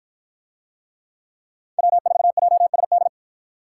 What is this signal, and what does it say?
Morse code, heard as a keyed tone, about 53 wpm, 700 Hz. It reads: W4JSD